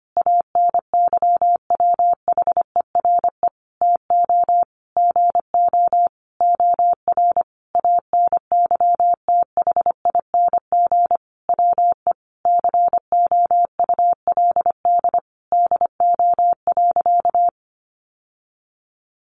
Morse code, heard as a keyed tone, about 25 wpm, 700 Hz. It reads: ANYWHERE TO GO OR ANYTHING WE COULD DO.
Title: Sentences from Top 1400 Words 25wpm